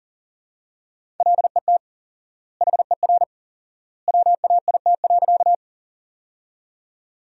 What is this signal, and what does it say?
Morse code, heard as a keyed tone, about 37 wpm, 700 Hz. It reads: LET HER WAIT.
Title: 4x Word-Spacing - Sentences from Top 500 Words - 40wpm